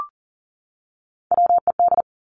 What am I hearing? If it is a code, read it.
Morse code, heard as a keyed tone, about 42 wpm, 700 Hz. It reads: WED
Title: Three Challenging Letter-Number Combinations 40wpm